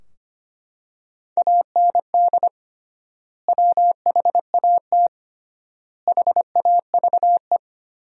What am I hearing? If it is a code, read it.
Morse code, heard as a keyed tone, about 25 wpm, 700 Hz. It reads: AND WHAT HAVE
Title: 3x Word-Spacing - Sentences from Top 500 Words - 25wpm